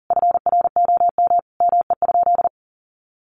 Morse code, heard as a keyed tone, about 40 wpm, 700 Hz. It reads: FROM ME?
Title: Sentences from Top 1500 Words 40wpm